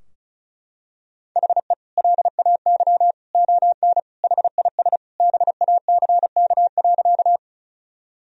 Morse code, heard as a keyed tone, about 34 wpm, 700 Hz. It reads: HE LAY ON HIS BACK.
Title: Sentences from Top 1000 Words - Review Part 2 - 35wpm